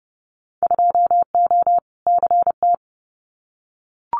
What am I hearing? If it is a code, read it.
Morse code, heard as a keyed tone, about 30 wpm, 700 Hz. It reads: 2O CT
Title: Winter Field Day Exchanges 30wpm